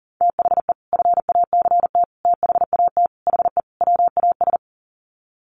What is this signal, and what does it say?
Morse code, heard as a keyed tone, about 40 wpm, 700 Hz. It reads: THE FACT THAT HE WAS
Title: Sets of 5 Words 40wpm